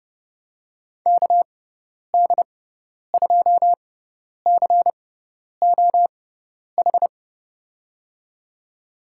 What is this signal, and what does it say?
Morse code, heard as a keyed tone, about 29 wpm, spaced as for 10 wpm, 700 Hz. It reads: KD2COH
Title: USA Call Signs - 6 Characters - 6x Character Spacing - 30wpm